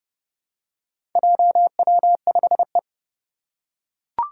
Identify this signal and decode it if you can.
Morse code, heard as a keyed tone, about 30 wpm, 700 Hz. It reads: JW5E
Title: Four Challenging Letter-Number Combinations 30wpm